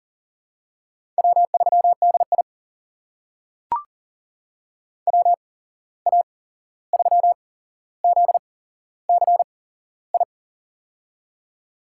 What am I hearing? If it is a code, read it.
Morse code, heard as a keyed tone, about 40 wpm, spaced as for 15 wpm, 700 Hz. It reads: W3DI WA3ZCI
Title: USA Call Signs - 8x Character Spacing - 40wpm